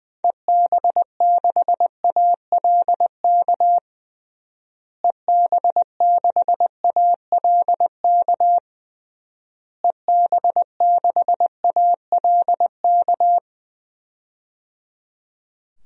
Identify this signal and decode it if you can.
Morse code, heard as a keyed tone, about 20 wpm, 700 Hz. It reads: EB6ALK EB6ALK EB6ALK
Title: DXCC-100 Call Signs - 6 Characters - Repeated 3x - 20wpm